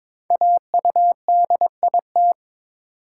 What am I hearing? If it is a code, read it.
Morse code, heard as a keyed tone, about 22 wpm, 700 Hz. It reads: AUDIT
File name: 5 Letter Words - First 3 Letters Given - 22wpm